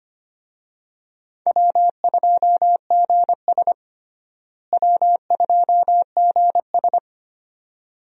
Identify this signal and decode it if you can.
Morse code, heard as a keyed tone, about 25 wpm, 700 Hz. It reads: W2GS W2GS
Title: USA Call Signs - 4 Characters - Repeated 2x - 25wpm